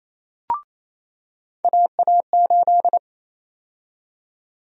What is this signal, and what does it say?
Morse code, heard as a keyed tone, about 28 wpm, 700 Hz. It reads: AA8
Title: USA Call Sign Prefixes - 28wpm